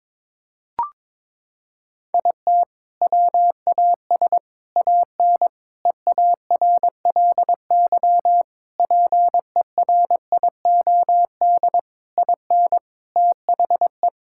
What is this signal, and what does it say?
Morse code, heard as a keyed tone, about 21 wpm, 700 Hz. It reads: IT WAS AN EARLY PERIOD IN THE
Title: Sentences from Top 1000 Words - Review Part 5 - 22wpm